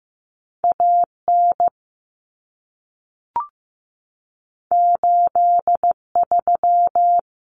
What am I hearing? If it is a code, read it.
Morse code, heard as a keyed tone, about 15 wpm, 700 Hz. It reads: AN 83
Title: Common Cut Numbers - 2-Digit 15wpm